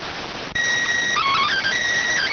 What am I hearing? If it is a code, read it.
SSTV (slow-scan television), PD 290.